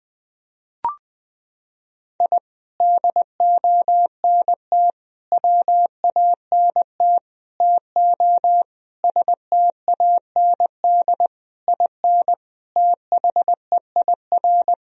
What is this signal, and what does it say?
Morse code, heard as a keyed tone, about 21 wpm, 700 Hz. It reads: I DONT WANT TO STAND IN THEIR
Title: Sentences from Top 1000 Words - Review Part 1 - 20wpm